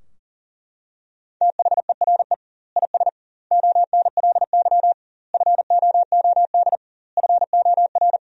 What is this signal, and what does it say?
Morse code, heard as a keyed tone, about 40 wpm, 700 Hz. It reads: THERE IS ONLY FOOD FOR
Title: 2x Word-Spacing - Sentences from Top 300 Words - 40wpm